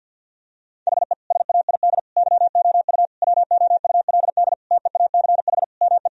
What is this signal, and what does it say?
Morse code, heard as a keyed tone, about 48 wpm, 700 Hz. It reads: HE SAID YOU WOULD TEACH ME
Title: Sentences from Top 1000 Words - Review Part 2 - 50wpm